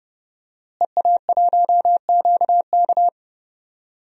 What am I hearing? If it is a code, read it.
Morse code, heard as a keyed tone, about 30 wpm, 700 Hz. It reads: EA1QK
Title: DXCC-100 Call Signs - 5 Characters - Repeated 3x - 30wpm